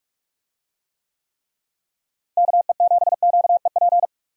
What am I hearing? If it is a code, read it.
Morse code, heard as a keyed tone, about 41 wpm, 700 Hz. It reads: KE7QEP
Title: USA Call Signs - 6 Characters - Repeated 3x - 45wpm